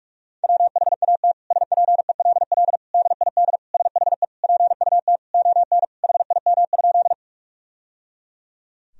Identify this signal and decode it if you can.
Morse code, heard as a keyed tone, about 46 wpm, 700 Hz. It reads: WHAT SPELL DID SHE PUT ON HIM?
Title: Sentences from Top 1000 Words - Review Part 1 - 45wpm